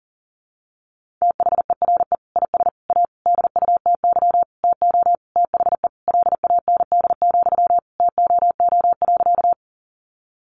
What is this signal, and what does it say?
Morse code, heard as a keyed tone, about 37 wpm, 700 Hz. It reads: THERE IS A DUTY TO THE LAND, TOO.
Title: Sentences from Top 1500 Words 40wpm